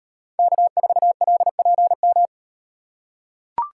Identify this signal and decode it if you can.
Morse code, heard as a keyed tone, about 35 wpm, 700 Hz. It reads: K4LPM